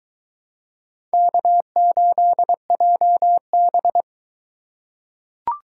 Morse code, heard as a keyed tone, about 21 wpm, 700 Hz. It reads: K8JB